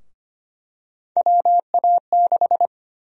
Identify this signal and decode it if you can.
Morse code, heard as a keyed tone, about 25 wpm, 700 Hz. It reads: WA6
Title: USA Call Sign Prefixes - Repeated 2x - 25wpm